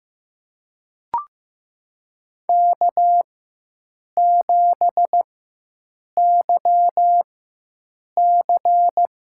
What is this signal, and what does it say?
Morse code, heard as a keyed tone, about 15 wpm, spaced as for 7 wpm, 700 Hz. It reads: K7YC
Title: USA Call Signs - 4 Characters - 4x Character Spacing - 15wpm